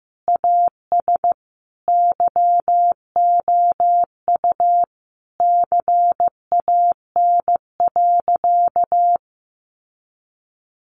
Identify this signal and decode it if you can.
Morse code, heard as a keyed tone, about 15 wpm, 700 Hz. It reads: AS YOU CAN.
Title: Sentences from Top 1200 Words 15wpm